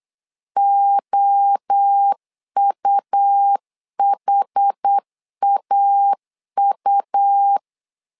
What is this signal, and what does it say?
Morse code, heard as a keyed tone, about 8 wpm, 800 Hz. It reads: OUHAU
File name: LEZIONI / 4 Q1S 8wpm 2710